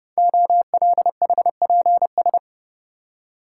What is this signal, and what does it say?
Morse code, heard as a keyed tone, about 30 wpm, 700 Hz. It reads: OLHPS
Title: Code Groups - 5 Letters - 30wpm